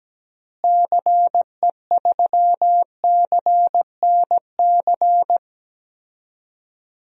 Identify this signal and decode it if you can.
Morse code, heard as a keyed tone, about 17 wpm, 700 Hz. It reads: CE3CNC
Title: DXCC-100 Call Signs - 6 Characters - Repeated 3x - 17wpm